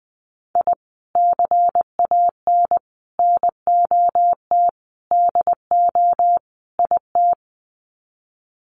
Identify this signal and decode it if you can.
Morse code, heard as a keyed tone, about 20 wpm, 700 Hz. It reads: I CAN NOT DO IT
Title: Sets of 5 Words Encore 20wpm